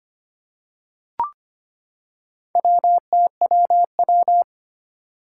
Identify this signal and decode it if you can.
Morse code, heard as a keyed tone, about 25 wpm, 700 Hz. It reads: WTWW
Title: Four Challenging Letter-Number Combinations 25wpm